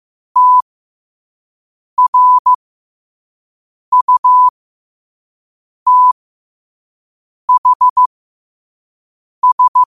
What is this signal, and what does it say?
Morse code, heard as a keyed tone, about 15 wpm, spaced as for 5 wpm, 1000 Hz. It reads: TRUTHS